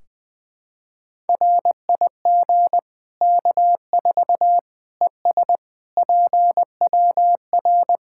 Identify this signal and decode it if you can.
Morse code, heard as a keyed tone, about 20 wpm, 700 Hz. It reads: RIG K4 ES PWR
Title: Practice QSO - Exchange 2 - Info - 20wpm